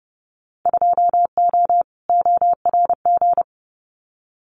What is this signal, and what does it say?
Morse code, heard as a keyed tone, about 30 wpm, 700 Hz. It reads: 2O ORG
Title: Winter Field Day Exchanges 30wpm